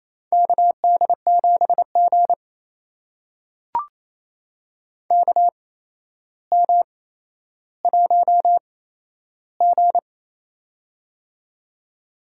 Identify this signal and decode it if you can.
Morse code, heard as a keyed tone, about 28 wpm, spaced as for 12 wpm, 700 Hz. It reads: KD7G KM1G
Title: USA Call Signs - 4 Characters - 8x Character Spacing - 28wpm